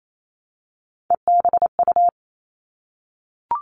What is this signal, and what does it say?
Morse code, heard as a keyed tone, about 28 wpm, 700 Hz. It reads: EBU
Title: All Cut Numbers - 3-Digit 28wpm